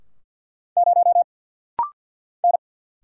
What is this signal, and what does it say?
Morse code, heard as a keyed tone, about 50 wpm, 690 Hz. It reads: 0 N
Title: Single Letter-Number Alt - Rapid-Fire 50wpm